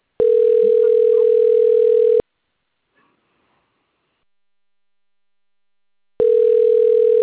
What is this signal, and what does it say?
Ringback tone (North America), 440 + 480 Hz.
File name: rg-1600-unknown- - - .96128